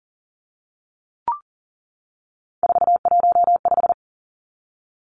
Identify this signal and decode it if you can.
Morse code, heard as a keyed tone, about 40 wpm, 700 Hz. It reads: 415
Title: Numbers - 3-Digit 40wpm